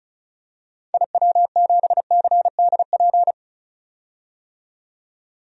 Morse code, heard as a keyed tone, about 36 wpm, 700 Hz. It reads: IW7CDP
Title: DXCC-100 Call Signs - 6 Characters - 35wpm